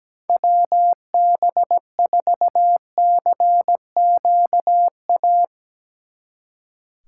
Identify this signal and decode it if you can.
Morse code, heard as a keyed tone, about 18 wpm, 700 Hz. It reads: WB4CQA
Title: Speed Racing - USA 2x3 Call Signs 17wpm